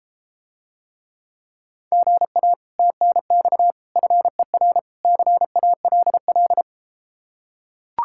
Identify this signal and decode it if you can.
Morse code, heard as a keyed tone, about 34 wpm, 700 Hz. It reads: GA TNX FER CALL